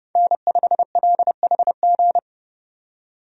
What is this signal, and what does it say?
Morse code, heard as a keyed tone, about 30 wpm, 700 Hz. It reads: N5LHG
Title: USA Call Signs - 5 Characters - 30wpm